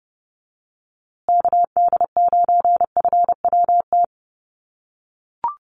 Morse code, heard as a keyed tone, about 30 wpm, 700 Hz. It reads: KD9FWT
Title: USA Call Signs Repeated 2x 30wpm